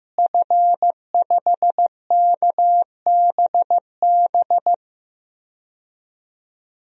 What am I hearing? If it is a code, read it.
Morse code, heard as a keyed tone, about 15 wpm, 700 Hz. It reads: F5KBB
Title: DXCC-100 Call Signs - 5 Characters - Repeated 3x - 15wpm